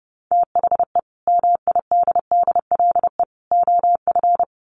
Morse code, heard as a keyed tone, about 28 wpm, 700 Hz. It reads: THE MIDDLE OF
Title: Sets of 3 Words 30wpm